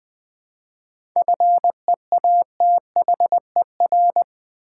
Morse code, heard as a keyed tone, about 20 wpm, 700 Hz. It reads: FEATHER
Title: 7 Letter Words - First 3 Letters Given - 20wpm